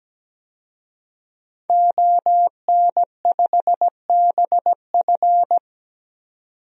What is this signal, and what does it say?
Morse code, heard as a keyed tone, about 17 wpm, 700 Hz. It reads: ON5BF
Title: DXCC-100 Call Signs - 5 Characters - 17wpm